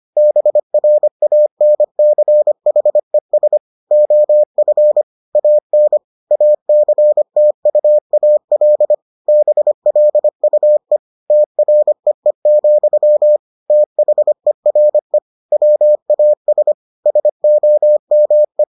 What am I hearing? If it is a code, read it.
Morse code, heard as a keyed tone, about 25 wpm, 600 Hz. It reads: BRANCHES OF AN ACTUAL BLUE TREE, THERE WAS SOME